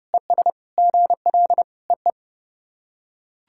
Morse code, heard as a keyed tone, about 29 wpm, 700 Hz. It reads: ES GL EE
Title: SOTA QSO Practice 30wpm